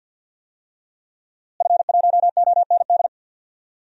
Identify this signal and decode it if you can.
Morse code, heard as a keyed tone, about 46 wpm, 700 Hz. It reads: F1OND
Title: DXCC-100 Call Signs - 5 Characters - Repeated 2x - 50wpm